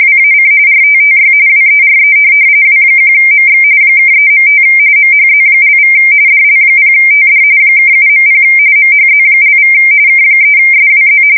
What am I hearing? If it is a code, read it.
RTTY, about 45.45 baud. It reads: WAC RTTY WERE W1MX (3.5 MHZ)= DL0TD (7.0 MHZ)= K3SWZ
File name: fsk170 / teletype-RTTY